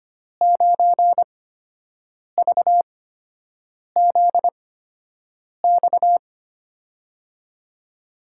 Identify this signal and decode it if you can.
Morse code, heard as a keyed tone, about 25 wpm, spaced as for 7 wpm, 700 Hz. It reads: 9VZX
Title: USA Call Signs - 8x Character Spacing - 25wpm